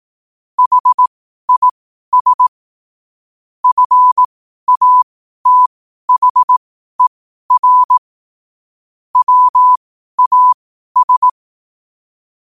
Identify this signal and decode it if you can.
Morse code, heard as a keyed tone, about 18 wpm, spaced as for 12 wpm, 1000 Hz. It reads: HIS FATHER WAS